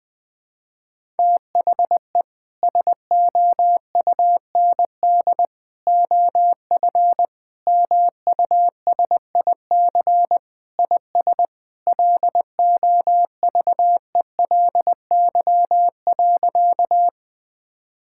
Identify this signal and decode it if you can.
Morse code, heard as a keyed tone, about 21 wpm, 700 Hz. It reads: THE SOUND OF MUSIC IS LOVELY.
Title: Sentences with 3-Word Phrases - 20wpm